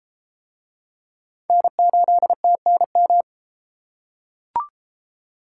Morse code, heard as a keyed tone, about 31 wpm, 700 Hz. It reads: N8TNM